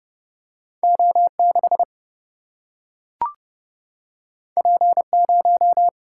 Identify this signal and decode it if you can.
Morse code, heard as a keyed tone, about 30 wpm, 700 Hz. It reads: O 6 P 0
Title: Single-Letter Single-Number 30wpm